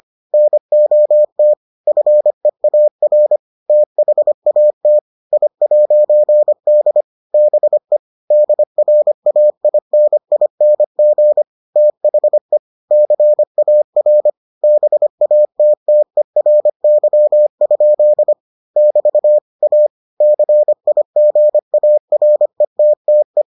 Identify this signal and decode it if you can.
Morse code, heard as a keyed tone, about 25 wpm, 600 Hz. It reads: NOT FEAR THAT I'D BE DRAINING THE CAR BATTERY? <BT> A CIGARETTE